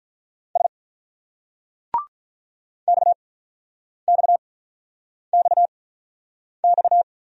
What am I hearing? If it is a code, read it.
Morse code, heard as a keyed tone, about 49 wpm, spaced as for 9 wpm, 700 Hz. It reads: S XXXX